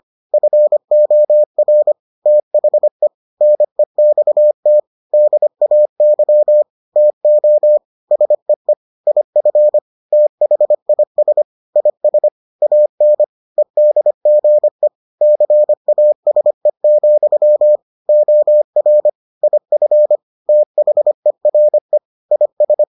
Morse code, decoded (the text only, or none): FOR THE NEXT DAY TO SEE IF THIS IS AN EDGE CASE, OR IF THERE IS